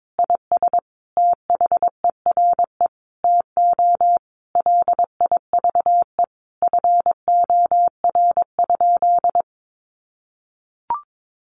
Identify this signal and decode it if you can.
Morse code, heard as a keyed tone, about 22 wpm, 700 Hz. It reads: IS THERE TO LIVE FOR?
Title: Sentences from Top 200 Words 22wpm